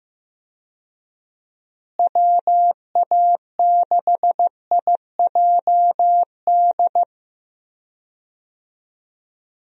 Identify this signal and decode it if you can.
Morse code, heard as a keyed tone, about 16 wpm, 700 Hz. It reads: WA6IJD
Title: USA Call Signs - 6 Characters - Repeated 3x - 15wpm